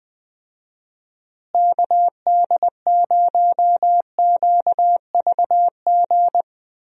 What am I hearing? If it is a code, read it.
Morse code, heard as a keyed tone, about 20 wpm, 700 Hz. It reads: KD0QVG
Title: USA Call Signs - 6 Characters - Repeated 3x - 20wpm